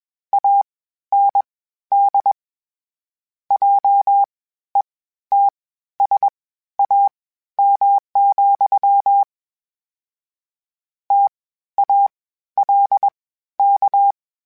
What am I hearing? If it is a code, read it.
Morse code, heard as a keyed tone, about 21 wpm, spaced as for 13 wpm, 800 Hz. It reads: AND JETSAM, TALK
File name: Misc / Misc-mp3s / WordPairs sp @20